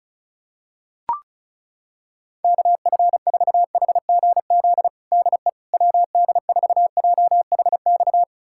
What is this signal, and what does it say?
Morse code, heard as a keyed tone, about 35 wpm, 700 Hz. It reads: KF4HGZ DE WD4JHX
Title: USA Callsign DE Callsign - 6 Characters - 35wpm